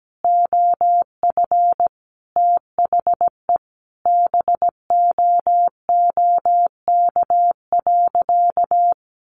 Morse code, heard as a keyed tone, about 17 wpm, 700 Hz. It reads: OF THE BOOK.
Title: Sentences from Top 1200 Words 17wpm